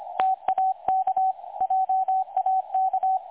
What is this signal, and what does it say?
Morse code, heard as a keyed tone, about 25 wpm, 760 Hz. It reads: TAK JAK
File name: websdr recording 2018-03-25T20 27 14Z 3565.0kHz